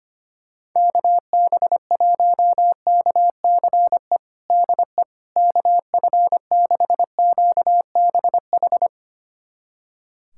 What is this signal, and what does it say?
Morse code, heard as a keyed tone, about 25 wpm, 700 Hz. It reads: KB1KCE DE KF6QBH
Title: USA Callsign DE Callsign 25wpm